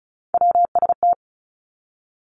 Morse code, heard as a keyed tone, about 36 wpm, 700 Hz. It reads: WST
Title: Three Challenging Letter-Number Combinations 35wpm